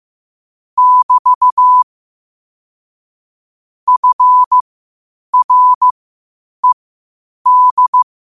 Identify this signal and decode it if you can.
Morse code, heard as a keyed tone, about 15 wpm, spaced as for 8 wpm, 1000 Hz. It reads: <BT> FRED
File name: qotdfiles / 8wpm